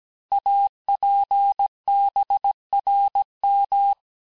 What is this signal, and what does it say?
Morse code, heard as a keyed tone, about 17 wpm, 800 Hz. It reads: APBRM
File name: cours-du-lundi files / cours-1587-texte-chiffre-vitesse-16MPM